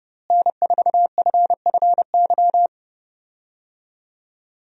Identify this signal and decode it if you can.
Morse code, heard as a keyed tone, about 30 wpm, 700 Hz. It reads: N4FFY
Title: USA Call Signs - 5 Characters - Repeated 3x - 30wpm